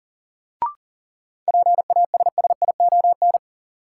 Morse code, heard as a keyed tone, about 38 wpm, 700 Hz. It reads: PASSION
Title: Sentences from Top 1300 Words - New Word Drill - 40wpm